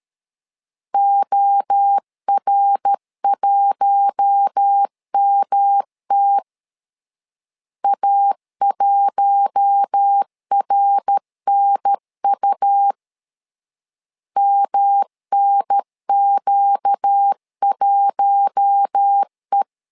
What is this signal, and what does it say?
Morse code, heard as a keyed tone, about 13 wpm, 800 Hz. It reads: OR1MT A1RNU MNQ1E
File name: LEZIONI / 4 Q1 12wpm 2028